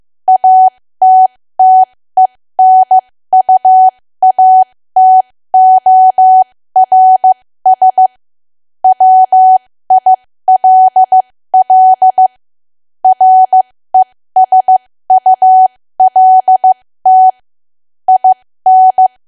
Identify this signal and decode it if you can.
Morse code, heard as a keyed tone, about 15 wpm, spaced as for 13 wpm, 750 Hz. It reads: ATTENUATORS WILL RESULT IN
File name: cwa / W1AW-Level2 / 13WPM